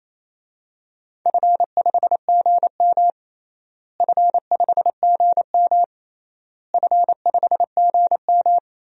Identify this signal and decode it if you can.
Morse code, heard as a keyed tone, about 28 wpm, 700 Hz. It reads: F5GM F5GM F5GM
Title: DXCC-100 Call Signs - 4 Characters - Repeated 3x - 28wpm